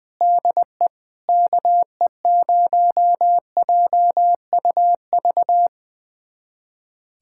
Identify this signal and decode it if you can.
Morse code, heard as a keyed tone, about 21 wpm, 700 Hz. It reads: DE KE0JUV
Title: USA Callsign DE Callsign 20wpm